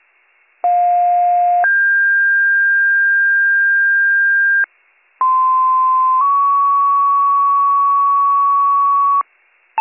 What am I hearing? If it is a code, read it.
Two-tone page, heard as a one-second tone, then a three-second tone, about 710 then 1600 Hz, about 1040 then 1120 Hz. It is heard from 0.6 to 9.2 s.